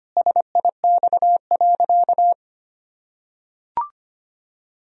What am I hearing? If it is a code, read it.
Morse code, heard as a keyed tone, about 25 wpm, 700 Hz. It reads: SIX.
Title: Sentences from Top 1000 Words - Review Part 3 - 25wpm